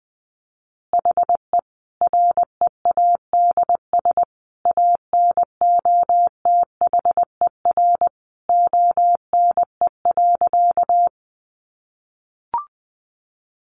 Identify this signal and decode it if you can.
Morse code, heard as a keyed tone, about 20 wpm, 700 Hz. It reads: HE READS ANOTHER ONE.
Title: Additional Easy Phrases 20wpm